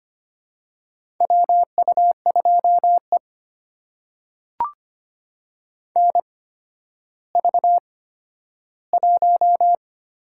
Morse code, heard as a keyed tone, about 25 wpm, spaced as for 12 wpm, 700 Hz. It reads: WU2E NV1
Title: USA Call Signs - 4 Characters - 8x Character Spacing - 25wpm